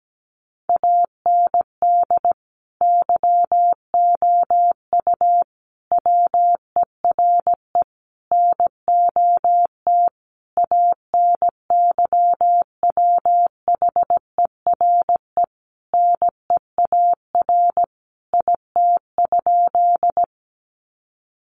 Morse code, heard as a keyed tone, about 17 wpm, 700 Hz. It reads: AND YOU WERE NOT ANYWHERE NEAR IT?
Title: Sentences from Top 1400 Words 17wpm